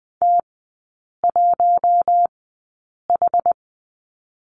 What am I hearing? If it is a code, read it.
Morse code, heard as a keyed tone, about 20 wpm, spaced as for 8 wpm, 700 Hz. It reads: T1H
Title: 107 Lesson 18e - Letter m - Three Characters 20wpm